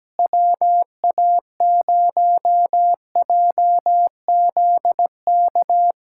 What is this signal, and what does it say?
Morse code, heard as a keyed tone, about 17 wpm, 700 Hz. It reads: WA0JZK
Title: K1USN SST QSO Practice - 17wpm